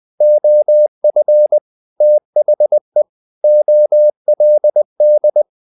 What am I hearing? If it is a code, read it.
Morse code, heard as a keyed tone, about 20 wpm, 600 Hz. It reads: OF THE OLD
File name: the old man and the sea / the old man and the sea part 1 20wpm